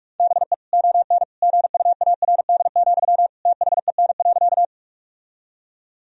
Morse code, heard as a keyed tone, about 46 wpm, 700 Hz. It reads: BE ON GUARD, THEN.
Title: Sentences from Top 1000 Words - Review Part 4 - 45wpm